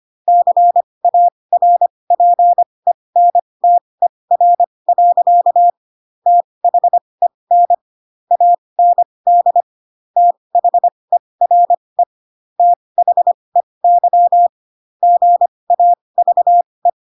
Morse code, heard as a keyed tone, about 25 wpm, spaced as for 20 wpm, 700 Hz. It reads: CARPENTER. THEN AND THERE THEY GAVE
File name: Pinocchio / Pinocchio ch2 15 25 ik1hge